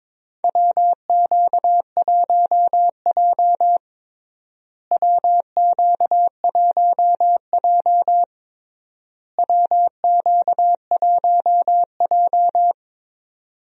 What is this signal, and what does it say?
Morse code, heard as a keyed tone, about 22 wpm, 700 Hz. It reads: WQ1J WQ1J WQ1J
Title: USA Call Signs - 4 Characters - Repeated 3x - 22wpm